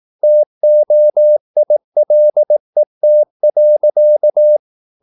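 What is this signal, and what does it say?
Morse code, heard as a keyed tone, about 18 wpm, 600 Hz. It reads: TOILET.
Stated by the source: Slow (QRS) Morse practice at 18 wpm, CW med Gnister.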